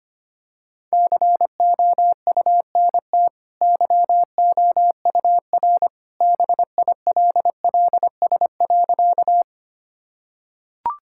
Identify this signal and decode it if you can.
Morse code, heard as a keyed tone, about 25 wpm, 700 Hz. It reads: COUNT YOUR BILLS.
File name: Easy Sentences with Hints - 25wpm